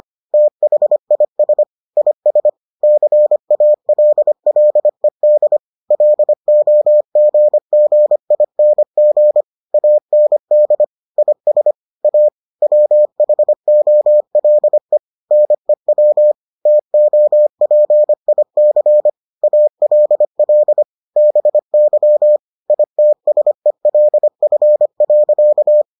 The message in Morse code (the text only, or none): THIS IS CALLED LOGGING AND IS A WHOLE NEW TOPIC ALL BY ITSELF.